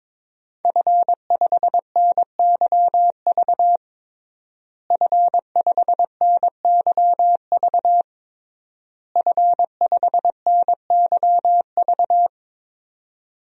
Morse code, heard as a keyed tone, about 22 wpm, 700 Hz. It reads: F5NYV F5NYV F5NYV
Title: DXCC-100 Call Signs - 5 Characters - Repeated 3x - 22wpm